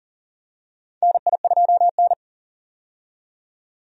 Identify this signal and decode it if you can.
Morse code, heard as a keyed tone, about 40 wpm, 700 Hz. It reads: NI2N
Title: USA Call Signs - 4 Characters - 40wpm